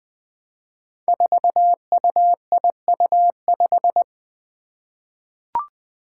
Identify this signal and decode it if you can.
Morse code, heard as a keyed tone, about 20 wpm, 700 Hz. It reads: 4UIU5
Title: Five Challenging Letter-Number Combinations 20wpm